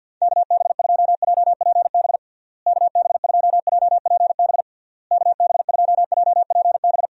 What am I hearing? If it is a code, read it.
Morse code, heard as a keyed tone, about 51 wpm, 700 Hz. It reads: KB2JPB KB2JPB KB2JPB
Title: USA Call Signs - 6 Characters - Repeated 3x - 50wpm